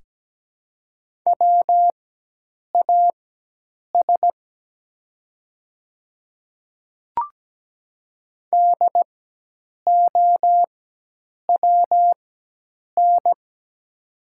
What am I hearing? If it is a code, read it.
Morse code, heard as a keyed tone, about 17 wpm, spaced as for 8 wpm, 700 Hz. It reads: WAS DOWN
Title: 4x Character Spacing - Top 100 Words - 17wpm